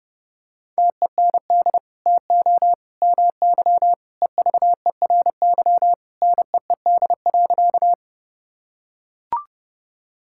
Morse code, heard as a keyed tone, about 30 wpm, 700 Hz. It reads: TEND TO MY EVERY NEED.
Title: Sentences from Top 1000 Words - Review Part 5 - 30wpm